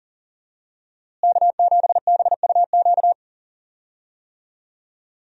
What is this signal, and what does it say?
Morse code, heard as a keyed tone, about 40 wpm, 700 Hz. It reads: K7BUQ
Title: USA Call Signs - 5 Characters - Repeated 2x - 40wpm